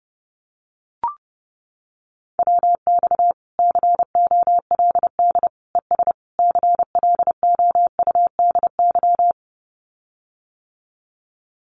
Morse code, heard as a keyed tone, about 30 wpm, 700 Hz. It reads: WX COLD ES CLOUDY
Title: Common QSO Phrases Encore 30wpm